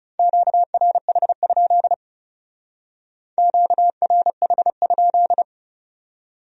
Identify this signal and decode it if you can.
Morse code, heard as a keyed tone, about 36 wpm, 700 Hz. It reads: QRH? QRH?
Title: Speed-Racing Q-Codes - 30wpm